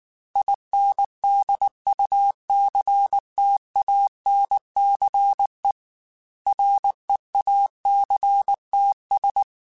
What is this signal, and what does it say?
Morse code, heard as a keyed tone, about 19 wpm, 800 Hz. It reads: INDUCTANCE REACTS
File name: Technical Subjects / Technical Subjects-mp3s / Reactance @18